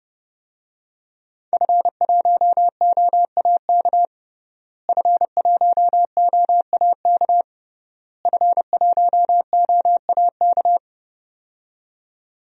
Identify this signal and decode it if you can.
Morse code, heard as a keyed tone, about 32 wpm, 700 Hz. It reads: F1OAK F1OAK F1OAK
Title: DXCC-100 Call Signs - 5 Characters - Repeated 3x - 30wpm